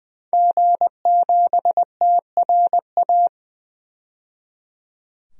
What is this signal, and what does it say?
Morse code, heard as a keyed tone, about 21 wpm, 700 Hz. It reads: G7TRA
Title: DXCC-100 Call Signs - 5 Characters - 20wpm